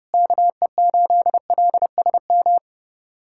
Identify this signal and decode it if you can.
Morse code, heard as a keyed tone, about 30 wpm, 700 Hz. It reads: KE8LSM
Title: USA Callsign DE Callsign - 6 Characters - 30wpm